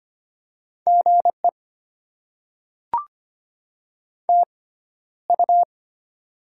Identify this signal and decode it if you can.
Morse code, heard as a keyed tone, about 25 wpm, spaced as for 13 wpm, 690 Hz. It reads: GE TU
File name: Top 20 QSO Elements - 6x Character Spacing - 25wpm